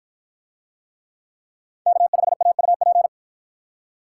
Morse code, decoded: K5AUP

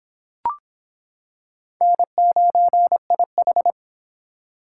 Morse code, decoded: N9IH

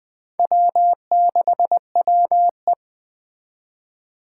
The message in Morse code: W6WE

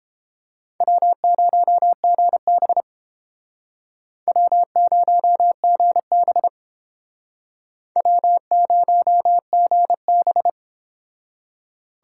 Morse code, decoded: W0GB W0GB W0GB

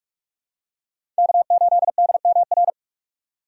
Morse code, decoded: K8DMR

